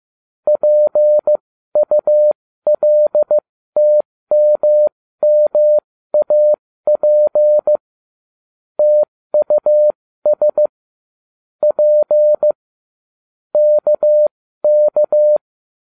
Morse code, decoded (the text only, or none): PULTMMAP TUS P KK